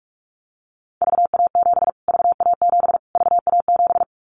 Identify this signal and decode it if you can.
Morse code, heard as a keyed tone, about 44 wpm, 700 Hz. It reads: VA7 VA7 VA7
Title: Callsign Prefixes for Top 25 DXCC - Repeated 3x - 45wpm